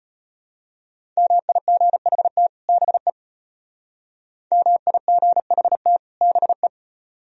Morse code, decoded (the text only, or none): MIGHT BE MIGHT BE